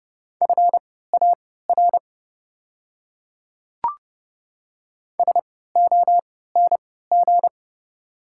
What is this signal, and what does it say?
Morse code, heard as a keyed tone, about 30 wpm, spaced as for 17 wpm, 700 Hz. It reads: FAR SONG
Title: 3x Character Spacing - Top 500 Words - 30wpm